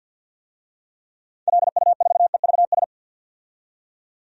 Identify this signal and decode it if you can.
Morse code, heard as a keyed tone, about 51 wpm, 700 Hz. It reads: LU4EVS
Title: DXCC-100 Call Signs - 6 Characters - 50wpm